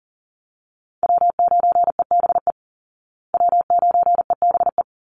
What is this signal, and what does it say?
Morse code, heard as a keyed tone, about 40 wpm, 700 Hz. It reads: W9EBE W9EBE
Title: Call Signs Repeated 3x - 40wpm